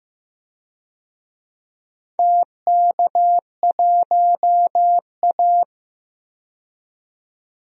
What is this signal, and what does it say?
Morse code, heard as a keyed tone, about 15 wpm, 700 Hz. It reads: TK1A
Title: DXCC-100 Call Signs - 4 Characters - 15wpm